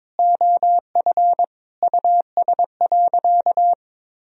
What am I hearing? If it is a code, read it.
Morse code, heard as a keyed tone, about 22 wpm, 700 Hz. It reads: OF US.
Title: Sentences from Top 1000 Words - Review Part 1 - 22wpm